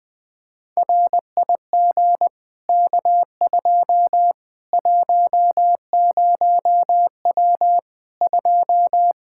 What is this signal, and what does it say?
Morse code, heard as a keyed tone, about 20 wpm, 700 Hz. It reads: RIG K2 10W 2
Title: Practice QSO - Exchange 2 - Info - 20wpm